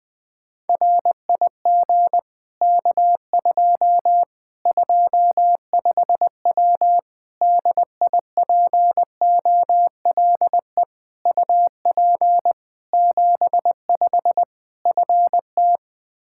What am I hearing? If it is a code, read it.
Morse code, heard as a keyed tone, about 21 wpm, 700 Hz. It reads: RIG K2 25W DIPOLE UP 75 FT
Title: Practice QSO - Exchange 2 - Info - 20wpm